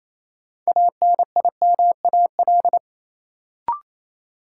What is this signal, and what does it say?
Morse code, heard as a keyed tone, about 28 wpm, 700 Hz. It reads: ANIMAL
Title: Sentences from Top 1000 Words - Review Part 1 - New Word Drill - 28wpm